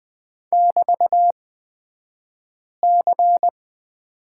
Morse code, heard as a keyed tone, about 20 wpm, spaced as for 5 wpm, 700 Hz. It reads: <BT>C
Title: 223 Lesson 36d - BT - Two Characters 20wpm